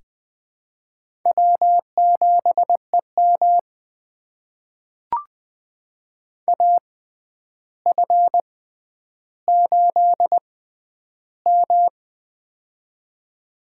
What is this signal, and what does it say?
Morse code, heard as a keyed tone, about 20 wpm, spaced as for 10 wpm, 700 Hz. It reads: W7EM AF8M
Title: USA Call Signs - 4 Characters - 6x Character Spacing - 20wpm